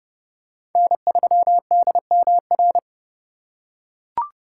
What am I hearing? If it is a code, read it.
Morse code, heard as a keyed tone, about 29 wpm, 700 Hz. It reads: N3DMR